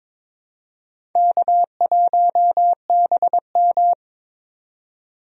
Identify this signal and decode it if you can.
Morse code, heard as a keyed tone, about 22 wpm, 700 Hz. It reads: K1BM
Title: USA Call Signs - 4 Characters - 22wpm